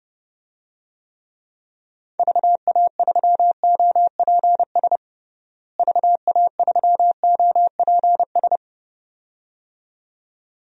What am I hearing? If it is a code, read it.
Morse code, heard as a keyed tone, about 32 wpm, 700 Hz. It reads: VA3OPS VA3OPS
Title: DXCC-100 Call Signs - 6 Characters - Repeated 2x - 30wpm